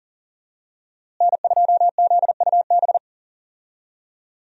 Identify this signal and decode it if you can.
Morse code, heard as a keyed tone, about 40 wpm, 700 Hz. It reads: N2ZUB
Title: USA Call Signs - 5 Characters - Repeated 3x - 40wpm